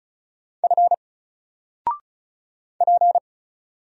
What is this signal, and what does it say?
Morse code, heard as a keyed tone, about 35 wpm, 700 Hz. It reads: F P
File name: Single-Letter-Number-35wpm